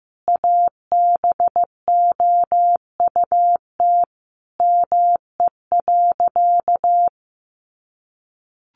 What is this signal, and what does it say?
Morse code, heard as a keyed tone, about 14 wpm, 700 Hz. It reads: ABOUT ME.
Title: Sentences from Top 1200 Words 15wpm